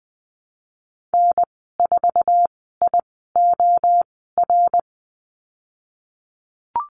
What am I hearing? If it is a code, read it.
Morse code, heard as a keyed tone, about 20 wpm, spaced as for 15 wpm, 700 Hz. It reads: N4IOR
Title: 088 Lesson 15g - Number 5 - Call Signs 20wpm